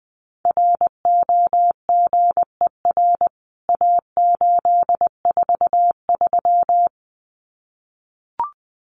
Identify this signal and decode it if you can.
Morse code, heard as a keyed tone, about 20 wpm, 700 Hz. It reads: ROGER A843
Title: CWT Exchange 20wpm